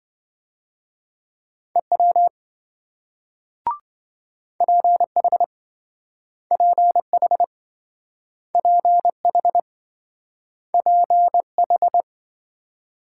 Morse code, heard as a keyed tone, about 27 wpm, 700 Hz. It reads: EW PH PH PH PH